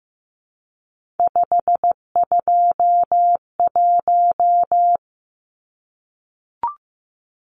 Morse code, heard as a keyed tone, about 15 wpm, 700 Hz. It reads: 521
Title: Numbers - 3-Digit 15wpm